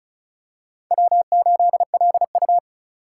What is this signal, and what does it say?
Morse code, heard as a keyed tone, about 36 wpm, 700 Hz. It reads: W8LU
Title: USA Call Signs - 4 Characters - 2x Character Spacing - 35wpm